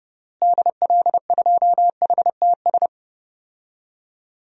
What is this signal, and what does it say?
Morse code, heard as a keyed tone, about 32 wpm, 700 Hz. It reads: DL2HTS
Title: DXCC-100 Call Signs - 6 Characters - Repeated 3x - 30wpm